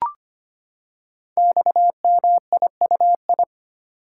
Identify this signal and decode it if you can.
Morse code, heard as a keyed tone, about 25 wpm, 700 Hz. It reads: XMIUI
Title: Code Groups - 5 Letters - 25wpm